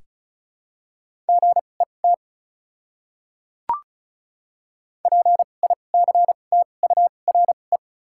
Morse code, heard as a keyed tone, about 36 wpm, spaced as for 25 wpm, 700 Hz. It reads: GET PICTURE
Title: 2x Character Spacing - Top 200 Words - 35wpm